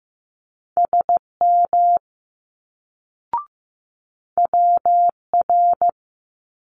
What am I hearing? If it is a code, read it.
Morse code, heard as a keyed tone, about 15 wpm, 700 Hz. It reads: SM WR
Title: Multi-letter Phonemes Quick Fire 15wpm